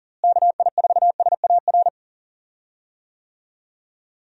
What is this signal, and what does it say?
Morse code, heard as a keyed tone, about 40 wpm, 700 Hz. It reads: KI4SAR
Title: USA Call Signs - 6 Characters - Repeated 3x - 40wpm